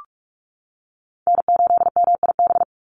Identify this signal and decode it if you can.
Morse code, heard as a keyed tone, about 44 wpm, 700 Hz. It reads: N8MIB
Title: USA Call Signs 45wpm